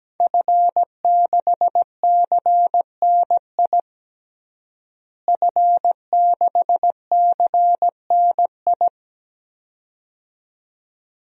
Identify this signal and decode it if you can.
Morse code, heard as a keyed tone, about 17 wpm, 700 Hz. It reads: F6CNI F6CNI
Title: DXCC-100 Call Signs - 5 Characters - Repeated 2x - 17wpm